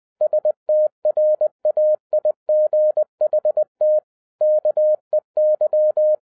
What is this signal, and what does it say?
Morse code, heard as a keyed tone, about 20 wpm, 610 Hz. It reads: STRAIGHT KEY